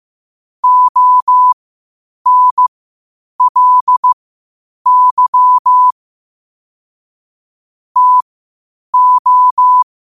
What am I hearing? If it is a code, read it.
Morse code, heard as a keyed tone, about 15 wpm, spaced as for 8 wpm, 1000 Hz. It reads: ONLY TO
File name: qotdfiles / 8wpm